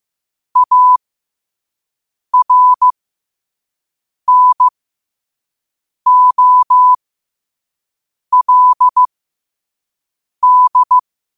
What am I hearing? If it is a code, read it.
Morse code, heard as a keyed tone, about 15 wpm, spaced as for 5 wpm, 1000 Hz. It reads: ARNOLD